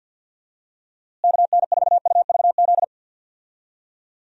Morse code, heard as a keyed tone, about 46 wpm, 700 Hz. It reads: KN4UVZ
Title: USA Call Signs - 6 Characters - Repeated 2x - 50wpm